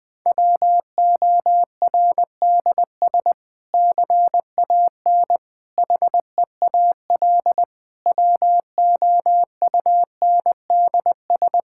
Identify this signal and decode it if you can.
Morse code, heard as a keyed tone, about 20 wpm, 700 Hz. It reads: WORDS CAN HEAL WOUNDS
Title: Difficult Sentences with Hints - 20wpm